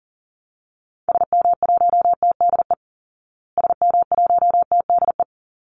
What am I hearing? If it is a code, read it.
Morse code, heard as a keyed tone, about 40 wpm, 700 Hz. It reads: SM1TDE SM1TDE
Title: Call Signs Repeated 3x - 40wpm